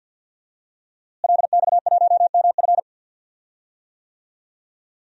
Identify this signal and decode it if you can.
Morse code, heard as a keyed tone, about 51 wpm, 700 Hz. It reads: LX1MF
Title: DXCC-100 Call Signs - 5 Characters - Repeated 2x - 50wpm